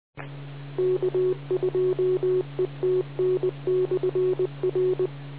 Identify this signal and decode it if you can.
Morse code, heard as a keyed tone, about 20 wpm, 370 Hz. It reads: K2ETN/R